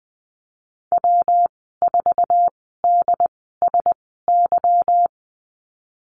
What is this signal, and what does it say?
Morse code, heard as a keyed tone, about 20 wpm, spaced as for 15 wpm, 700 Hz. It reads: W4DSY
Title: 167 Lesson 27g - Number 7 - Call Signs 20wpm